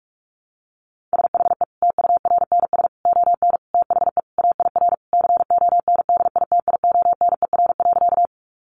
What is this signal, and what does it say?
Morse code, heard as a keyed tone, about 44 wpm, 700 Hz. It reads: SHE TURNS ON THE AIR CONDITIONER.
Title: Easy Phrases 45wpm